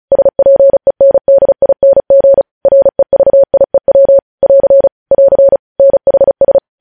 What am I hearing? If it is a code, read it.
Morse code, heard as a keyed tone, about 36 wpm, 550 Hz. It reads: SPENDING REVIEW <AR> <AR> NHS
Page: news headlines in morse code at 35 wpm